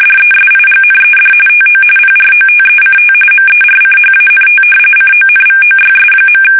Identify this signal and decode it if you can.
RTTY, about 45.45 baud. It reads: DESIGNED TO PROTECT MERCHANT CONVOYS.
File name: fsk850